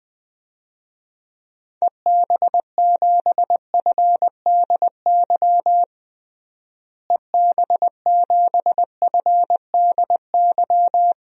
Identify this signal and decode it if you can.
Morse code, heard as a keyed tone, about 20 wpm, 700 Hz. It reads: EB7FDY EB7FDY
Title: DXCC-100 Call Signs - 6 Characters - Repeated 3x - 20wpm